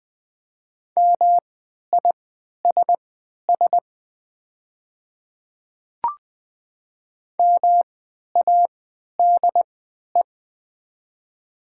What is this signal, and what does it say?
Morse code, heard as a keyed tone, about 20 wpm, spaced as for 11 wpm, 700 Hz. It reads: MISS MADE
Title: 3x Character Spacing - Top 500 Words - 20wpm